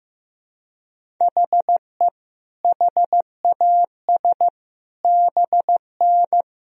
Morse code, heard as a keyed tone, about 15 wpm, 700 Hz. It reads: HE HAS BN
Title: Sentences from Top 300 Words and Abbreviations 15wpm